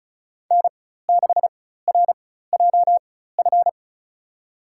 Morse code, heard as a keyed tone, about 34 wpm, spaced as for 16 wpm, 700 Hz. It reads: N6RJF
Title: USA Call Signs - 5 Characters - 4x Character Spacing - 35wpm